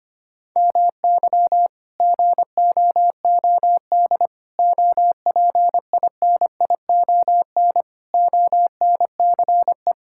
Morse code, heard as a keyed tone, about 25 wpm, 700 Hz. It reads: MY GOOD OPINION ONCE
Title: Sentences from Top 1600 Words 25wpm